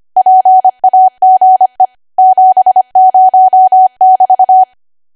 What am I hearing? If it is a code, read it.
Morse code, heard as a keyed tone, about 25 wpm, 750 Hz. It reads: PAGE 70<BT>
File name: cwa / W1AW-Level3 / 25WPM